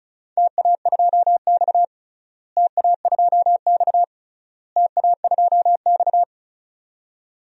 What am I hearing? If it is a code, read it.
Morse code, heard as a keyed tone, about 36 wpm, 700 Hz. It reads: TA2X TA2X TA2X
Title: DXCC-100 Call Signs - 4 Characters - Repeated 3x - 35wpm